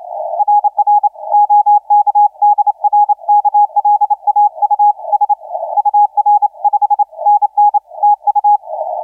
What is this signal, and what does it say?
Morse code, heard as a keyed tone, about 26 wpm, 830 Hz. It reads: RR OK DR KLAUS UR 5NN TU
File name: CT9 DK1AX 1922UTC 20m CW